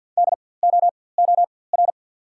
Morse code, heard as a keyed tone, about 54 wpm, spaced as for 24 wpm, 700 Hz. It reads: DOOR
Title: 4x Character Spacing - Top 500 Words - 50wpm